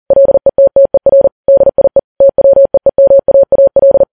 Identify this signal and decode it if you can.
Morse code, heard as a keyed tone, about 40 wpm, 550 Hz. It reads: LETTER DIE TWEEMAAL
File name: Files Web tls / Morse on9fv / morse40kps